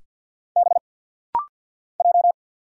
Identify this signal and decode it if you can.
Morse code, heard as a keyed tone, about 50 wpm, 700 Hz. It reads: B J
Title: Single Letters - Rapid-Fire 50wpm